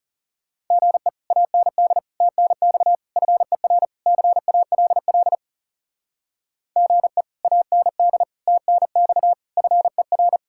Morse code, heard as a keyed tone, about 37 wpm, 700 Hz. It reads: GE AND TNX FER CALL GE AND TNX FER
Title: Speed-Racing Common QSO Phrases Encore - 30wpm